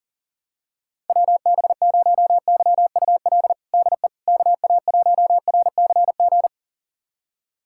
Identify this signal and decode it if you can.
Morse code, heard as a keyed tone, about 40 wpm, 700 Hz. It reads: WB0YUL DE KA1RCG
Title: USA Callsign DE Callsign 40wpm